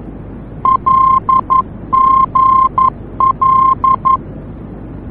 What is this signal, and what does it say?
Morse code, heard as a keyed tone, about 11 wpm, 1050 Hz. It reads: LGL
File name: france / mx / LGL-F-VOR